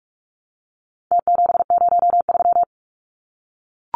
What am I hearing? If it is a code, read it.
Morse code, heard as a keyed tone, about 44 wpm, 700 Hz. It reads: T703
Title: Numbers - Contest Style - 4-Digit 45wpm